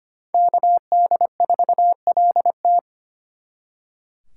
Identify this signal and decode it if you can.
Morse code, heard as a keyed tone, about 31 wpm, 700 Hz. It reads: KD4LT
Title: Speed Racing - USA 2x2 Call Signs 25wpm